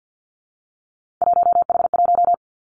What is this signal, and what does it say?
Morse code, heard as a keyed tone, about 50 wpm, 700 Hz. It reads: 1H1
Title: Three Challenging Letter-Number Combinations 50wpm